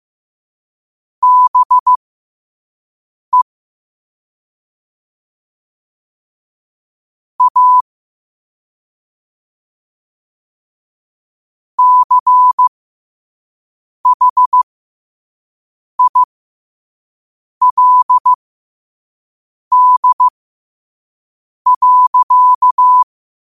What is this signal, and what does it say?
Morse code, heard as a keyed tone, about 15 wpm, spaced as for 5 wpm, 1000 Hz. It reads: BE A CHILD.
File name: qotdfiles / 5wpm